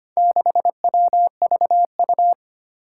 Morse code, heard as a keyed tone, about 25 wpm, 700 Hz. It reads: 6WVU
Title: Four Challenging Letter-Number Combinations 25wpm